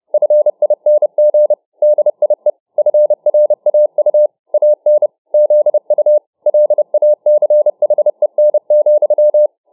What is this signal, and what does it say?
Morse code, heard as a keyed tone, about 29 wpm, 600 Hz. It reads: FING DIE FRAU AN ZU LACHEN,